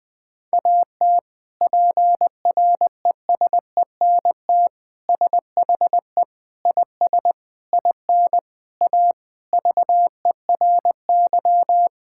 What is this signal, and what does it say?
Morse code, heard as a keyed tone, about 20 wpm, 700 Hz. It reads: AT PRESENT SHE IS IN A VERY
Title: Sentences from Top 1000 Words - Review Part 5 - 20wpm